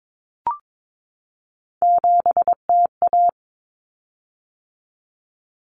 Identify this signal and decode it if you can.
Morse code, heard as a keyed tone, about 22 wpm, 700 Hz. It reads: 7TA
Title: Common Cut Numbers - 3-Digit 22wpm